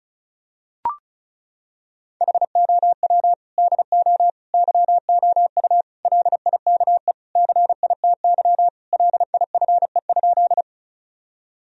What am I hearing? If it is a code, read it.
Morse code, heard as a keyed tone, about 36 wpm, 700 Hz. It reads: HOW DO YOU LIKE CITY LIFE?
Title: Sentences from Top 1000 Words - Review Part 1 - 35wpm